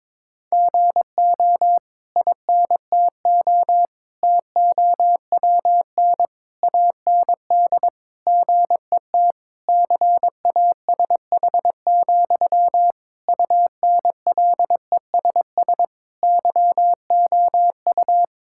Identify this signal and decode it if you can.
Morse code, heard as a keyed tone, about 21 wpm, 700 Hz. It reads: GO INTO TOWN AND GET CASH, UNLESS YOU
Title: Sentences from Top 1000 Words - Review Part 6 - 22wpm